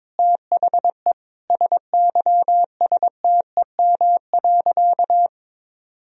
Morse code, heard as a keyed tone, about 22 wpm, 700 Hz. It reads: THE SYSTEM.
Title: Sentences from Top 1000 Words - Review Part 3 - 22wpm